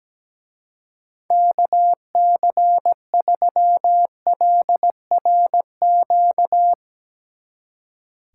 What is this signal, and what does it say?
Morse code, heard as a keyed tone, about 18 wpm, 700 Hz. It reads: KC3LRQ
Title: Speed Racing - USA 2x3 Call Signs 17wpm